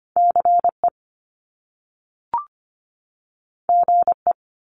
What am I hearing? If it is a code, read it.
Morse code, heard as a keyed tone, about 25 wpm, 700 Hz. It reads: C E G E
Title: Top Two Letter Combinations in Words 25wpm